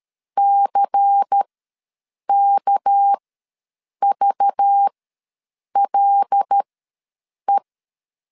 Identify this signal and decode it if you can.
Morse code, heard as a keyed tone, about 13 wpm, spaced as for 7 wpm, 800 Hz. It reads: CKVLE
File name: LEZIONI / 6 PKX0 / 6 PK ws 12wpm 3116